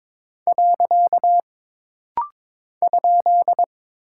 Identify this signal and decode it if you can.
Morse code, heard as a keyed tone, about 22 wpm, 700 Hz. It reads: . ?
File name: practice / 22wpm / Single Punctuation-Prosign - Rapid-Fire - 22wpm